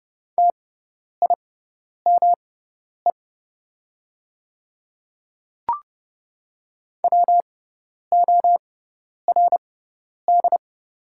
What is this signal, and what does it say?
Morse code, heard as a keyed tone, about 30 wpm, spaced as for 10 wpm, 700 Hz. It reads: TIME WORD
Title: 6x Character Spacing - Top 100 Words - 30wpm